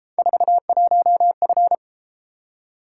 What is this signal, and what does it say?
Morse code, heard as a keyed tone, about 31 wpm, 700 Hz. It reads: 41F